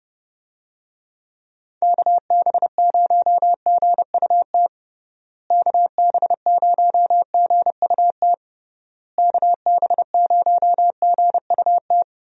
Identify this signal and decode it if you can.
Morse code, heard as a keyed tone, about 30 wpm, 700 Hz. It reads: KB0GUT KB0GUT KB0GUT
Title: USA Call Signs - 6 Characters - Repeated 3x - 30wpm